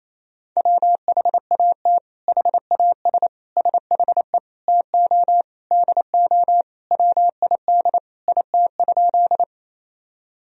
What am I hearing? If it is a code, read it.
Morse code, heard as a keyed tone, about 28 wpm, 700 Hz. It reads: WHAT HAS SHE TO DO WID IT?
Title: Sentences from Top 100 Words and Abbreviations 28wpm